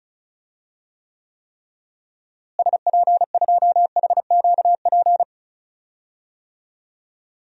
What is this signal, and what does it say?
Morse code, heard as a keyed tone, about 34 wpm, 700 Hz. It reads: SP2HQP
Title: DXCC-100 Call Signs - 6 Characters - Repeated 3x - 35wpm